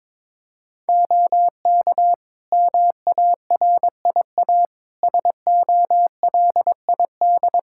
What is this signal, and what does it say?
Morse code, heard as a keyed tone, about 22 wpm, 700 Hz. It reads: OK MARIA SOLID
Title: Practice QSO - Exchange 3 - More Info - 22wpm